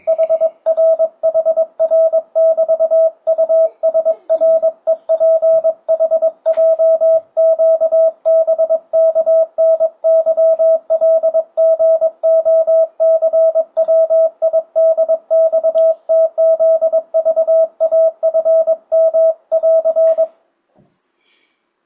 Morse code, heard as a keyed tone, about 22 wpm, 650 Hz. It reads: HR HR <BT> USREPHJQBKNYLGOCWIDXTZVAFM <AR>
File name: 2025cw / speedup / 22wpm